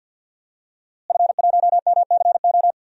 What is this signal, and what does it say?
Morse code, heard as a keyed tone, about 52 wpm, 700 Hz. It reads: F1MCO
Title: DXCC-100 Call Signs - 5 Characters - Repeated 2x - 50wpm